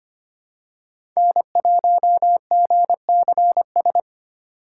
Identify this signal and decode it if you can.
Morse code, heard as a keyed tone, about 25 wpm, 700 Hz. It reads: N1GCS
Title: USA Call Signs - 3x Character Spacing - 25wpm